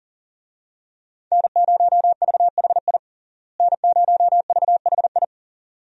Morse code, heard as a keyed tone, about 37 wpm, 700 Hz. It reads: N0VHI N0VHI
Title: USA Call Signs - 5 Characters - Repeated 2x - 40wpm